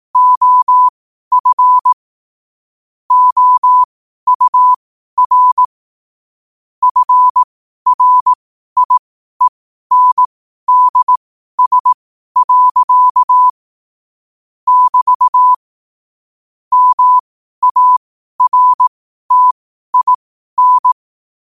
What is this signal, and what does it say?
Morse code, heard as a keyed tone, about 18 wpm, spaced as for 12 wpm, 1000 Hz. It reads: OF OUR FRIENDS. <BT> MARTIN
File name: qotdfiles / 12wpm